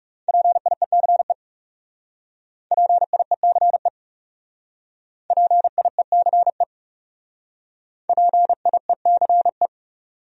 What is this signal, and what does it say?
Morse code, heard as a keyed tone, about 39 wpm, 700 Hz. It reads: PIECE PIECE PIECE PIECE